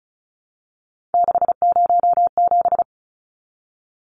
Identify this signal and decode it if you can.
Morse code, heard as a keyed tone, about 34 wpm, 700 Hz. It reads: 607
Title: Numbers - 3-Digit 35wpm